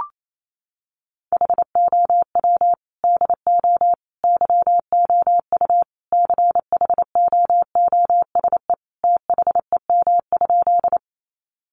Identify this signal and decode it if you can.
Morse code, heard as a keyed tone, about 28 wpm, 700 Hz. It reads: HOW DO YOU CHOOSE THEM?
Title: Sentences from Top 700 Words 28wpm